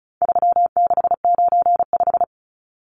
Morse code, heard as a keyed tone, about 34 wpm, 700 Hz. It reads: 3695
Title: Numbers - 4-Digit 35wpm